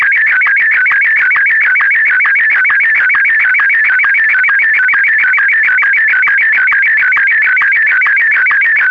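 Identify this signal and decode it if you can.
SSTV (slow-scan television), Martin 1.
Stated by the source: SSTV Martin M1 mode test image, 114 seconds.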